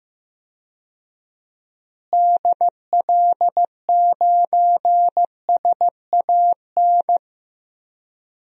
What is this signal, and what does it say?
Morse code, heard as a keyed tone, about 15 wpm, 700 Hz. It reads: DL9SAN
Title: DXCC-100 Call Signs - 6 Characters - Repeated 2x - 15wpm